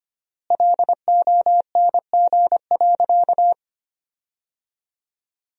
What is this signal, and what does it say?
Morse code, heard as a keyed tone, about 25 wpm, 700 Hz. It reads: LONG.
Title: 3x Word-Spacing - Sentences from Top 100 Words - 25wpm